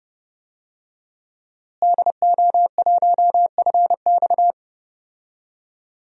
Morse code, heard as a keyed tone, about 30 wpm, 700 Hz. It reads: DO1FX
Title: DXCC-100 Call Signs - 5 Characters - Repeated 3x - 30wpm